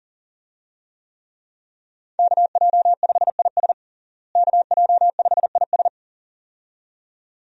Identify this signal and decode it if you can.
Morse code, heard as a keyed tone, about 40 wpm, 700 Hz. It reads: KJ5IS KJ5IS
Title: USA Call Signs - 5 Characters - Repeated 2x - 40wpm